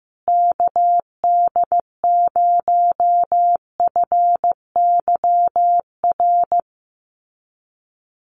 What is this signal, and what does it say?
Morse code, heard as a keyed tone, about 14 wpm, 700 Hz. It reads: KD0FYR
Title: USA Call Signs Repeated 2x 15wpm